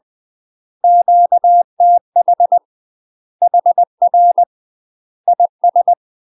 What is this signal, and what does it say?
Morse code, heard as a keyed tone, about 20 wpm, 700 Hz. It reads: QTH HR IS
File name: UK MASTER QSOs TRAINING / 20WPM / UK MASTER QSOs 0068